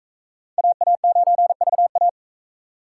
Morse code, heard as a keyed tone, about 42 wpm, 700 Hz. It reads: AA9VA